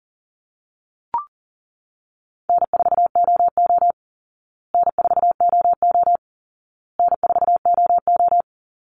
Morse code, heard as a keyed tone, about 40 wpm, 700 Hz. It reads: N4OO N4OO N4OO
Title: Easy Call Signs Repeated 3x - 40wpm